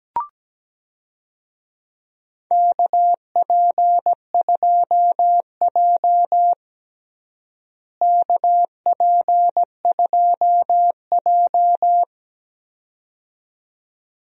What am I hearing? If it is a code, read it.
Morse code, heard as a keyed tone, about 17 wpm, 700 Hz. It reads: KP2J KP2J
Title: DXCC-100 Call Signs - 4 Characters - Repeated 2x - 17wpm